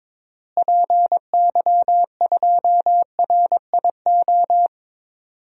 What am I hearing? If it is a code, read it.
Morse code, heard as a keyed tone, about 22 wpm, 700 Hz. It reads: PY2RIO
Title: DXCC-100 Call Signs - 6 Characters - Repeated 3x - 22wpm